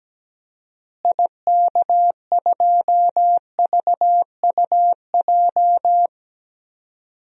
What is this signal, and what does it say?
Morse code, heard as a keyed tone, about 17 wpm, 700 Hz. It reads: IK2VUJ
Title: DXCC-100 Call Signs - 6 Characters - Repeated 3x - 17wpm